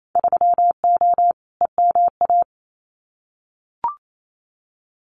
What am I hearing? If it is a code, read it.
Morse code, heard as a keyed tone, about 28 wpm, 700 Hz. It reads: 3O EMA
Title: Winter Field Day Exchanges 28wpm